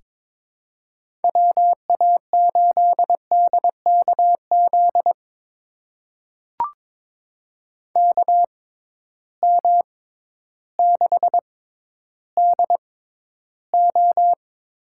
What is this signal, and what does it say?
Morse code, heard as a keyed tone, about 22 wpm, spaced as for 12 wpm, 700 Hz. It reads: WA8DKZ KM6DO
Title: USA Call Signs - 6 Characters - 6x Character Spacing - 22wpm